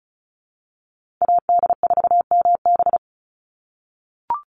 Morse code, heard as a keyed tone, about 34 wpm, 700 Hz. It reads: AD4MB
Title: USA Call Signs 35wpm